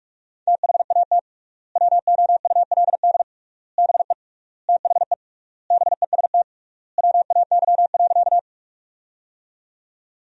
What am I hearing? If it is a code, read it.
Morse code, heard as a keyed tone, about 45 wpm, 700 Hz. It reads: THAT WOULD BE THE BEST WAY.
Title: 3x Word-Spacing - Sentences from Top 400 Words - 45wpm